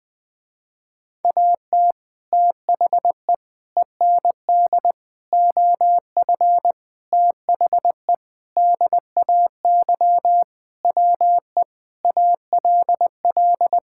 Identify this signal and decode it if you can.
Morse code, heard as a keyed tone, about 19 wpm, 700 Hz. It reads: AT THE END OF THE DAY WE ALL
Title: Sentences with 4-Word Phrases - Spoken Beginning - 20wpm